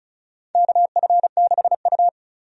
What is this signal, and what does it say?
Morse code, heard as a keyed tone, about 36 wpm, 700 Hz. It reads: KF6U
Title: USA Call Signs - 4 Characters - 35wpm